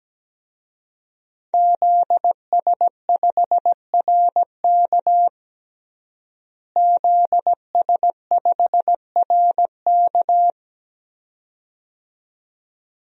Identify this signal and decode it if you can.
Morse code, heard as a keyed tone, about 17 wpm, 700 Hz. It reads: ZS5RK ZS5RK
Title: DXCC-100 Call Signs - 5 Characters - Repeated 2x - 17wpm